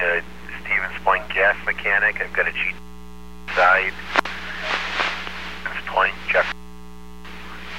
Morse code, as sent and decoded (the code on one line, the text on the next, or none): none